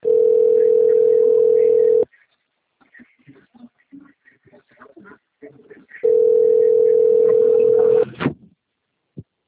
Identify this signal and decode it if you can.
Ringback tone (North America), 440 + 480 Hz, with speech.
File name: rg-1600-unknown- - - .5530